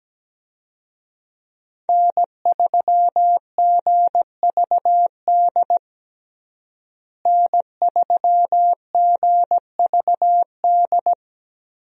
Morse code, heard as a keyed tone, about 17 wpm, 700 Hz. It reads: N3GVD N3GVD
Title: USA Call Signs - 5 Characters - Repeated 2x - 17wpm